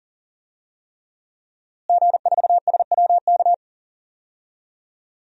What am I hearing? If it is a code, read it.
Morse code, heard as a keyed tone, about 40 wpm, 700 Hz. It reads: G4SWK
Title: DXCC-100 Call Signs - 5 Characters - Repeated 3x - 40wpm